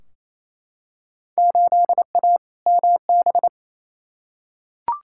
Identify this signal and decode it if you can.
Morse code, heard as a keyed tone, about 28 wpm, 700 Hz. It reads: 8A MB
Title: Field Day Exchanges Encore - 28wpm